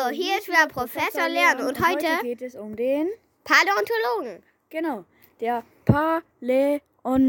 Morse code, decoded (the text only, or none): none